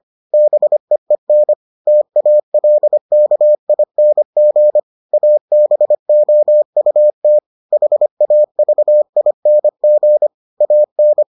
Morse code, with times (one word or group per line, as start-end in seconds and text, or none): 0.3-1.5 BEEN
1.9-4.8 TALKING
5.1-7.4 ABOUT
7.7-10.3 HAVING
10.6-11.2 AN